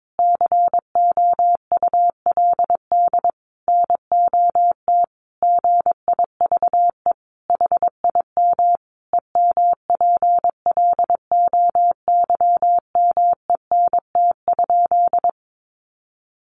Morse code, heard as a keyed tone, about 22 wpm, 700 Hz. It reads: COULD NOT GIVE HIM EMPLOYMENT?
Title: Sentences from Top 1500 Words 22wpm